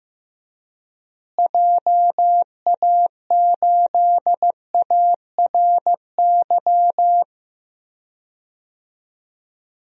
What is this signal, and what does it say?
Morse code, heard as a keyed tone, about 15 wpm, 700 Hz. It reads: JA8ARY
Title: DXCC-100 Call Signs - 6 Characters - Repeated 2x - 15wpm